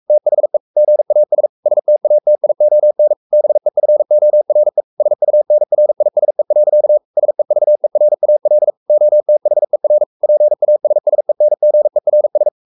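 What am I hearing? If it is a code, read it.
Morse code, heard as a keyed tone, about 44 wpm, 600 Hz. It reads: THE GAS STATION BEFORE SUNRISE. SEVERAL OTHER PASSENGERS
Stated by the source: Morse code at 43 wpm sending Ken Follett's novel Never in English, from CW med Gnister.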